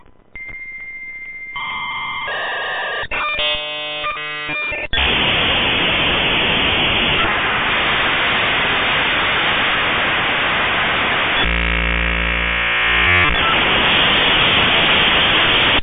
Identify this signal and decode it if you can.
Dial-up modem handshake.